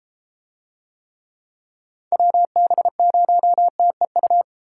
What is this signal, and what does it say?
Morse code, heard as a keyed tone, about 32 wpm, 700 Hz. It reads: WB0TEU